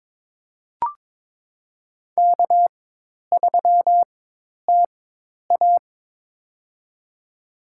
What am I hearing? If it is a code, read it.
Morse code, heard as a keyed tone, about 22 wpm, spaced as for 10 wpm, 700 Hz. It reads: K3TA
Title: USA Call Signs - 4 Characters - 4x Character Spacing - 22wpm